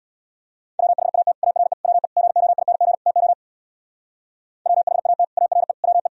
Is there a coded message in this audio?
Yes: Morse code from 0.8 to 6.1 s, heard as a keyed tone, about 64 wpm, 700 Hz. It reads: WHAT ARE WE GOING TO WHAT ARE WE